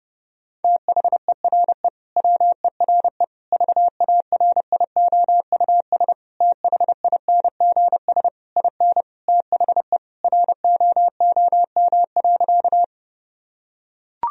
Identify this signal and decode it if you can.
Morse code, heard as a keyed tone, about 32 wpm, 700 Hz. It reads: THERE WERE VARIOUS THINGS IN THE ROOM.
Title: Sentences from Top 1000 Words - Review Part 4 - 30wpm